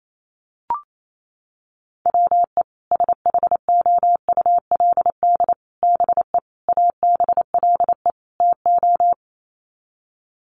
Morse code, heard as a keyed tone, about 28 wpm, 700 Hz. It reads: WE SHOULD BE ABLE TO
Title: Sets of 5 Words Encore2 28wpm